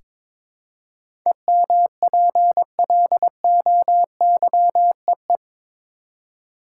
Morse code, heard as a keyed tone, about 22 wpm, 700 Hz. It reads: EMPLOYEE
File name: practice / 22wpm / Eight Letter Words - 22wpm